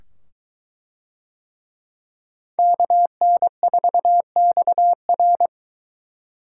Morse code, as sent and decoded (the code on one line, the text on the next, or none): -.- -. ....- -..- .-.
KN4XR